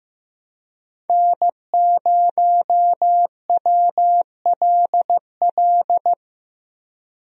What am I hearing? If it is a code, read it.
Morse code, heard as a keyed tone, about 15 wpm, 700 Hz. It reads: N0WLL
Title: USA Call Signs - 5 Characters - 15wpm